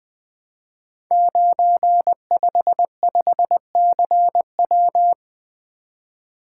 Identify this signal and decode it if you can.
Morse code, heard as a keyed tone, about 20 wpm, 700 Hz. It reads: 955CW
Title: Code Groups - 5 Letter:Numbers - 20wpm